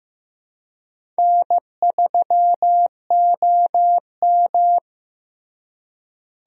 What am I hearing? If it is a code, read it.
Morse code, heard as a keyed tone, about 15 wpm, 700 Hz. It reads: N3OM
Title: USA Call Signs - 4 Characters - Repeated 3x - 15wpm